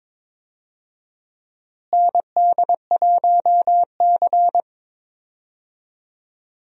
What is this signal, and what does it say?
Morse code, heard as a keyed tone, about 22 wpm, 700 Hz. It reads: ND1C
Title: USA Call Signs - 4 Characters - Repeated 3x - 22wpm